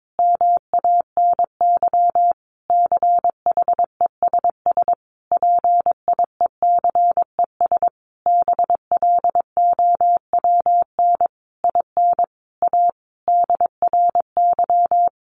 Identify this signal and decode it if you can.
Morse code, heard as a keyed tone, about 22 wpm, 700 Hz. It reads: MANY CHESS PIECES BLOWN IN A DRY
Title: A Sound Of Thunder 22wpm